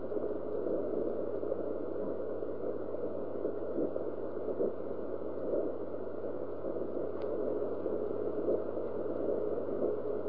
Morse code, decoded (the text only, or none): none